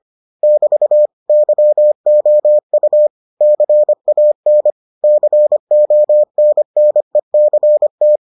Morse code, heard as a keyed tone, about 25 wpm, 600 Hz. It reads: <BT> YOU CAN CONNECT